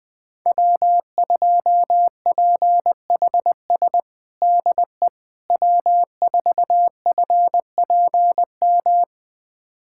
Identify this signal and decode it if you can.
Morse code, heard as a keyed tone, about 20 wpm, 700 Hz. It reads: W2PHS DE W4FPM
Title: USA Callsign DE Callsign - 5 Characters - 20wpm